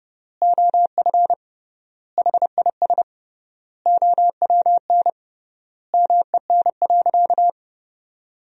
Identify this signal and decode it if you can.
Morse code, heard as a keyed tone, about 31 wpm, 700 Hz. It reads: OF HIS OWN MEN.
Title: 3x Word-Spacing - Sentences from Top 300 Words - 30wpm